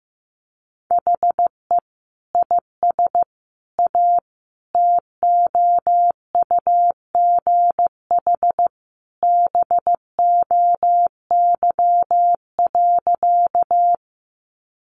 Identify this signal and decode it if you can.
Morse code, heard as a keyed tone, about 15 wpm, 700 Hz. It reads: HE IS A TOUGH BOY.
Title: Sentences from Top 1200 Words 15wpm